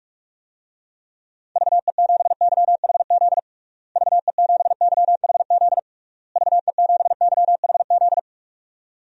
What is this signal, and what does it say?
Morse code, heard as a keyed tone, about 41 wpm, 700 Hz. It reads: VE7YHZ VE7YHZ VE7YHZ
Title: DXCC-100 Call Signs - 6 Characters - Repeated 3x - 45wpm